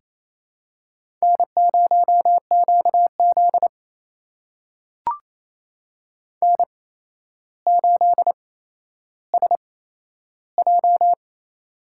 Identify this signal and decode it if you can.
Morse code, heard as a keyed tone, about 28 wpm, spaced as for 12 wpm, 700 Hz. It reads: N0QZ N8SJ
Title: USA Call Signs - 4 Characters - 8x Character Spacing - 28wpm